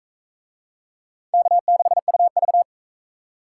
Morse code, heard as a keyed tone, about 36 wpm, 700 Hz. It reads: K6UV